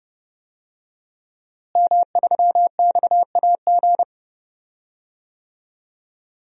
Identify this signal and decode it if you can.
Morse code, heard as a keyed tone, about 30 wpm, 700 Hz. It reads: M3XAG
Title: DXCC-100 Call Signs - 5 Characters - Repeated 3x - 30wpm